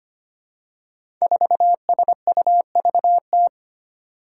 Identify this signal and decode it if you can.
Morse code, heard as a keyed tone, about 25 wpm, 700 Hz. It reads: 4SUVT
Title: Five Challenging Letter-Number Combinations 25wpm